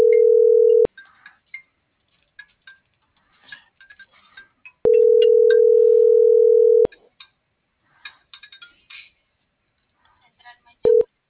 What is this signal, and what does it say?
Ringback tone (North America), 440 + 480 Hz.